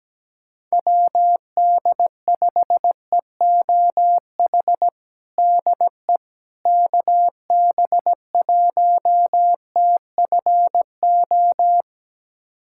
Morse code, heard as a keyed tone, about 17 wpm, 700 Hz. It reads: WD5EOH DE KB1TFO
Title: USA Callsign DE Callsign - 6 Characters - 17wpm